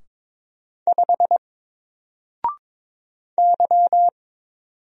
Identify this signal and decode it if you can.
Morse code, heard as a keyed tone, about 22 wpm, 700 Hz. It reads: 5 Y
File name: Single-Letter-Number-22wpm